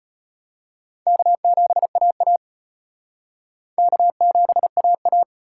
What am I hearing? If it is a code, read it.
Morse code, heard as a keyed tone, about 32 wpm, 700 Hz. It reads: K7AA K7AA